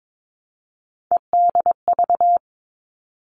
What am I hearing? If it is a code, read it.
Morse code, heard as a keyed tone, about 22 wpm, 700 Hz. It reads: EDV
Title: All Cut Numbers - 3-Digit 22wpm